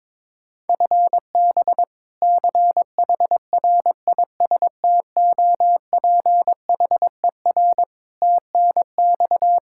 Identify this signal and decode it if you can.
Morse code, heard as a keyed tone, about 22 wpm, 700 Hz. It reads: FB CHRISTOPHER TNX
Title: K1USN SST QSO Practice 22wpm